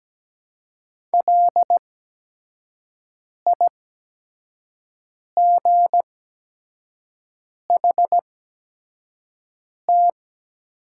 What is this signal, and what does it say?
Morse code, heard as a keyed tone, about 17 wpm, spaced as for 5 wpm, 700 Hz. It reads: LIGHT
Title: 8x Character Spacing - Top 300 Words - 17wpm